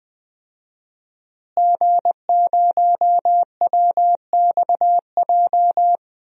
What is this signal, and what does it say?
Morse code, heard as a keyed tone, about 21 wpm, 700 Hz. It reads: G0WXJ
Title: DXCC-100 Call Signs - 5 Characters - Repeated 2x - 20wpm